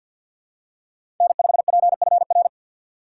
Morse code, heard as a keyed tone, about 48 wpm, 700 Hz. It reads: N5PFR
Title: USA Call Signs - 5 Characters - 6x Character Spacing - 50wpm